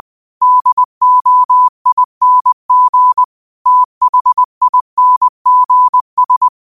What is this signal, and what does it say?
Morse code, heard as a keyed tone, about 20 wpm, 1000 Hz. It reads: DOING THINGS
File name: qotdfiles / 20wpm